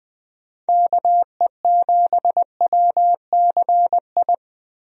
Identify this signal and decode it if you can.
Morse code, heard as a keyed tone, about 20 wpm, 700 Hz. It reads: KE7WCI
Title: USA Call Signs - 6 Characters - Repeated 3x - 20wpm